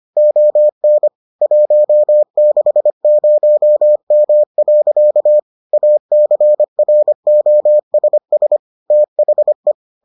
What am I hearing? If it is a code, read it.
Morse code, heard as a keyed tone, about 25 wpm, 600 Hz. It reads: ON 160M. ACROSS THE